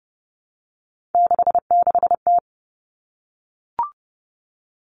Morse code, heard as a keyed tone, about 30 wpm, 700 Hz. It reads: 66T
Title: Common Cut Numbers - 3-Digit 30wpm